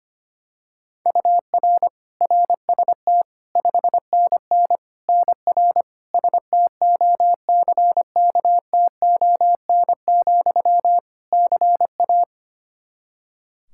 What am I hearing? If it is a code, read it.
Morse code, heard as a keyed tone, about 25 wpm, 700 Hz. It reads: UR RST 5NN NR STOCKTON, CA
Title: Practice QSO - Exchange 1 - Beginning - 25wpm